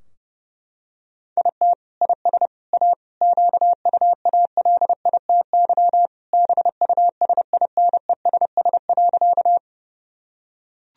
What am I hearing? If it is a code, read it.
Morse code, heard as a keyed tone, about 30 wpm, 700 Hz. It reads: IT IS A QUALITY BUSINESS.
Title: Sentences from Top 1000 Words - Review Part 4 - 30wpm